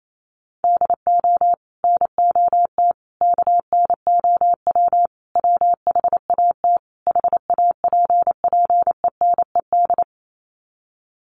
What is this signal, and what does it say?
Morse code, heard as a keyed tone, about 28 wpm, 700 Hz. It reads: DO NOT KNOW WHAT HAPPENED
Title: Sets of 5 Words Encore2 28wpm